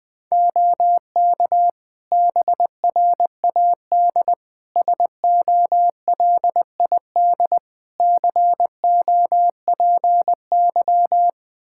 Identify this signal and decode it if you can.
Morse code, heard as a keyed tone, about 20 wpm, 700 Hz. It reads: OK BRAD SOLID COPY
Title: K1USN SST QSO Practice 20wpm